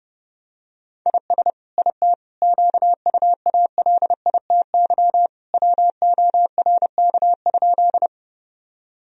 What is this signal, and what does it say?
Morse code, heard as a keyed tone, about 30 wpm, 700 Hz. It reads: IS IT QUALITY WORK?
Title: Sentences from Top 1000 Words - Review Part 4 - 30wpm